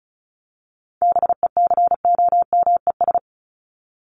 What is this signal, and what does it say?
Morse code, heard as a keyed tone, about 34 wpm, 700 Hz. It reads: BECOMES
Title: ES:S Suffix - Top 100 Words with Base Words 35wpm